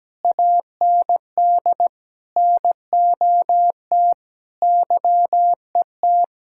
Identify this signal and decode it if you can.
Morse code, heard as a keyed tone, about 17 wpm, 700 Hz. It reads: AND NOT YET
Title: Sentences from Top 1000 Words - Review Part 6 - 17wpm